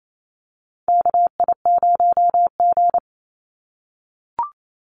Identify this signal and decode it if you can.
Morse code, heard as a keyed tone, about 28 wpm, 700 Hz. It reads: KI0G
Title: NA SOTA Chaser Callsigns 28wpm